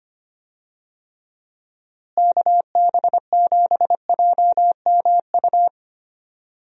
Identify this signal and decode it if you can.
Morse code, heard as a keyed tone, about 25 wpm, 700 Hz. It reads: KB7JMU
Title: USA Call Signs - 6 Characters - Repeated 3x - 25wpm